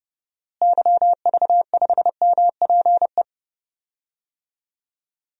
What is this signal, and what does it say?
Morse code, heard as a keyed tone, about 30 wpm, 700 Hz. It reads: YV5MPE
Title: DXCC-100 Call Signs - 6 Characters - Repeated 3x - 30wpm